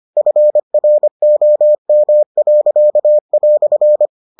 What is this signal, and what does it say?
Morse code, heard as a keyed tone, about 25 wpm, 600 Hz. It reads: FROM."